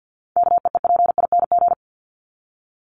Morse code, heard as a keyed tone, about 48 wpm, 700 Hz. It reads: KEEPING
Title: ING Suffix - Top 100 Words with Base Words 50wpm